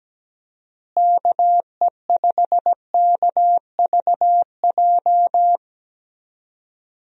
Morse code, heard as a keyed tone, about 17 wpm, 700 Hz. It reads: KE5KVJ
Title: USA Call Signs - 6 Characters - Repeated 3x - 17wpm